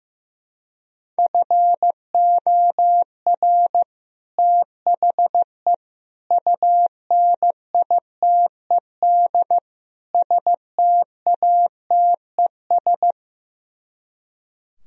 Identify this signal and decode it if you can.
Morse code, heard as a keyed tone, about 15 wpm, 700 Hz. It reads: FOR THE UNITED STATES
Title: Speed-Racing - Sets of 4 Words 15wpm